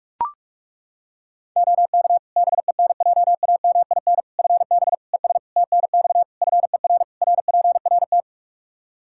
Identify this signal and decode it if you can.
Morse code, heard as a keyed tone, about 41 wpm, 700 Hz. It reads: OK BENJAMIN FB ES TNX FER RPRT
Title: K1USN SST QSO Practice 45wpm